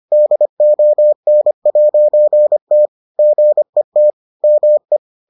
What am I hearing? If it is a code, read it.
Morse code, heard as a keyed tone, about 25 wpm, 600 Hz. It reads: DON'T GET ME